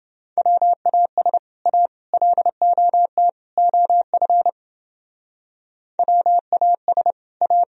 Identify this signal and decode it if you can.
Morse code, heard as a keyed tone, about 30 wpm, 700 Hz. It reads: WAS A LOT OF WAS A